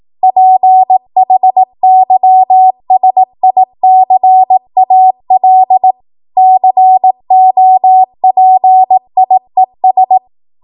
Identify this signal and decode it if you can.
Morse code, heard as a keyed tone, about 18 wpm, 750 Hz. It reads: PHYSICAL COPIES
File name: Morse / 18WPM